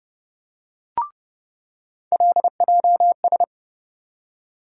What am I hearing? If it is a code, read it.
Morse code, heard as a keyed tone, about 32 wpm, 700 Hz. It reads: LJS
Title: USA Call Sign Postfixes - 3 Letters 30wpm